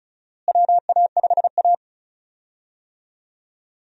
Morse code, heard as a keyed tone, about 36 wpm, 700 Hz. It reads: WA5A
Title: USA Call Signs - 4 Characters - Repeated 2x - 35wpm